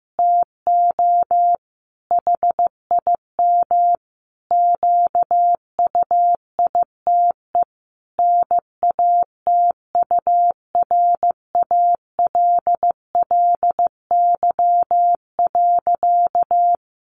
Morse code, heard as a keyed tone, about 15 wpm, 700 Hz. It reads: TO HIM QUITE NATURALLY.
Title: Sentences from Top 1300 Words 15wpm